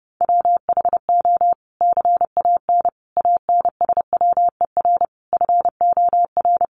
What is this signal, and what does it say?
Morse code, heard as a keyed tone, about 30 wpm, 700 Hz. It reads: WHO CAN ANSWER FOR
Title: Sentences from Top 300 Words 30wpm